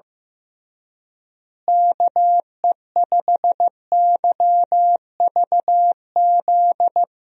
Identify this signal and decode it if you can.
Morse code, heard as a keyed tone, about 15 wpm, 700 Hz. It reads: KE5YVZ
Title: USA Call Signs - 6 Characters - Repeated 3x - 15wpm